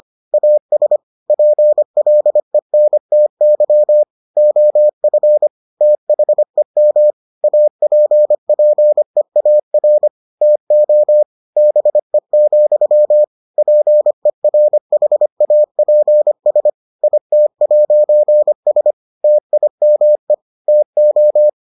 Morse code, heard as a keyed tone, about 25 wpm, 600 Hz. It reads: AS PLENTY OF THEM APPEAR TO BE, PERHAPS IT'S TIME TO